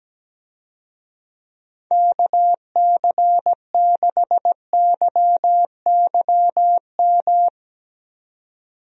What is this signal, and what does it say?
Morse code, heard as a keyed tone, about 17 wpm, 700 Hz. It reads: KC6YYM
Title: USA Call Signs - 6 Characters - Repeated 2x - 17wpm